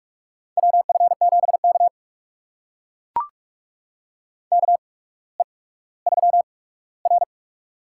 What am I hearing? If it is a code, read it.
Morse code, heard as a keyed tone, about 44 wpm, spaced as for 19 wpm, 700 Hz. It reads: WF7K KE3R
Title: USA Call Signs - 4 Characters - 8x Character Spacing - 45wpm